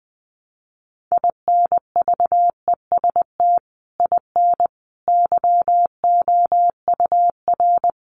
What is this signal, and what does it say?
Morse code, heard as a keyed tone, about 20 wpm, 700 Hz. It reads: INVEST IN YOUR
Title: Sentences from Top 1100 Words 20wpm